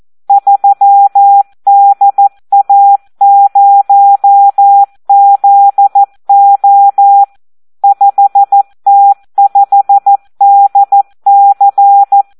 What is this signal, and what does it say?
Morse code, heard as a keyed tone, about 14 wpm, 800 Hz. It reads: 3DA0ZO 5T5DC
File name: Audio practica CW / SD 1 14